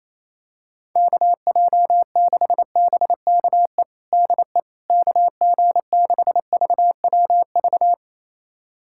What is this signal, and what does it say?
Morse code, heard as a keyed tone, about 27 wpm, 700 Hz. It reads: KJ6BKE DE KG6VWV
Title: USA Callsign DE Callsign - 6 Characters - 28wpm